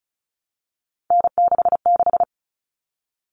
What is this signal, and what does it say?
Morse code, heard as a keyed tone, about 34 wpm, 700 Hz. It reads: N66
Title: All Cut Numbers - 3-Digit 35wpm